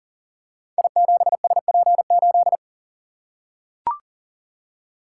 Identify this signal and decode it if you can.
Morse code, heard as a keyed tone, about 42 wpm, 690 Hz. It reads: I7SP8
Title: Code Groups - 5 Letter:Numbers - 40wpm